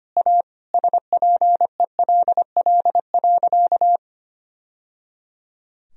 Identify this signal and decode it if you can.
Morse code, heard as a keyed tone, about 25 wpm, 700 Hz. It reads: A SPELL.
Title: Sentences from Top 1000 Words - Review Part 1 - 25wpm